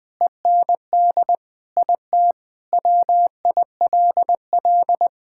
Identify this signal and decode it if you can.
Morse code, heard as a keyed tone, about 21 wpm, 700 Hz. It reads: END IT WILL
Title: Sentences with 3-Word Phrases - 20wpm